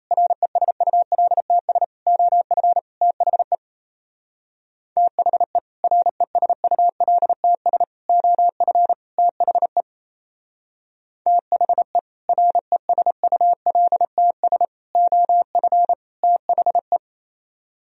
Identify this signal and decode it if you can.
Morse code, heard as a keyed tone, about 34 wpm, 700 Hz. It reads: RESULTS OF THE THE RESULTS OF THE THE RESULTS OF THE
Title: Speed-Racing - Sets of 4 Words 28wpm